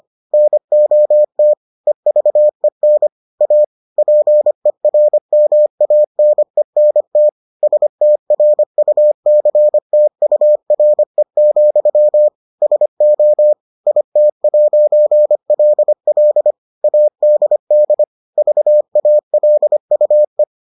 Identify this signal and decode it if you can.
Morse code, heard as a keyed tone, about 25 wpm, 600 Hz. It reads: NOT EVEN A PERMANENT STRUCTURE, SO IT'LL ADD VALUE